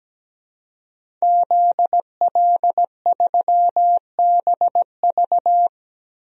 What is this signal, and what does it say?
Morse code, heard as a keyed tone, about 17 wpm, 700 Hz. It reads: ZL3BV
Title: DXCC-100 Call Signs - 5 Characters - Repeated 3x - 17wpm